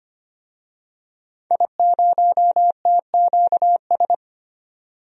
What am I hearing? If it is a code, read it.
Morse code, heard as a keyed tone, about 25 wpm, 700 Hz. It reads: I0TQS
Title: DXCC-100 Call Signs - 5 Characters - 25wpm